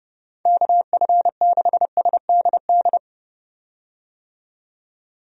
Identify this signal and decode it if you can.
Morse code, heard as a keyed tone, about 31 wpm, 700 Hz. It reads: KF6SDD
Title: USA Call Signs - 6 Characters - 30wpm